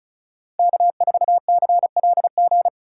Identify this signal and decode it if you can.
Morse code, heard as a keyed tone, about 43 wpm, 700 Hz. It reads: K4CLG
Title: Speed Racing - USA Call Signs 35wpm